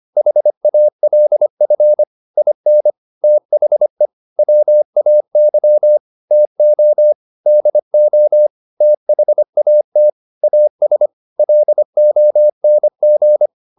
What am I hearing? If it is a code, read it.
Morse code, heard as a keyed tone, about 25 wpm, 600 Hz. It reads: HALF IN THE WAY TO DO THAT AS LONG
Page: advanced-practice-files